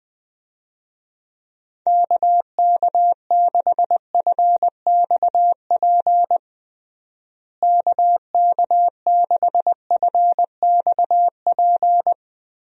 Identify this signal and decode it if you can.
Morse code, heard as a keyed tone, about 21 wpm, 700 Hz. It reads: KK6FXP KK6FXP
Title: USA Call Signs - 6 Characters - Repeated 2x - 20wpm